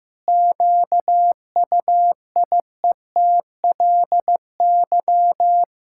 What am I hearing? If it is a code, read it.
Morse code, heard as a keyed tone, about 15 wpm, 700 Hz. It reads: QUIETLY
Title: Pangram Challenge v2 - 15wpm